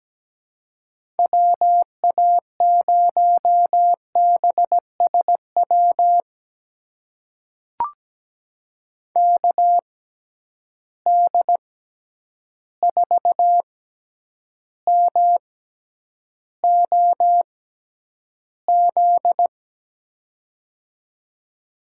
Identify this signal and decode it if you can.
Morse code, heard as a keyed tone, about 17 wpm, spaced as for 9 wpm, 700 Hz. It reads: WA0BSW KD4MOZ
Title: USA Call Signs - 6 Characters - 6x Character Spacing - 17wpm